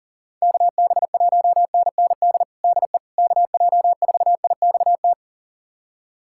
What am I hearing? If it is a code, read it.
Morse code, heard as a keyed tone, about 40 wpm, 700 Hz. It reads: KB1NND DE KJ4IXT
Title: USA Callsign DE Callsign - 6 Characters - 40wpm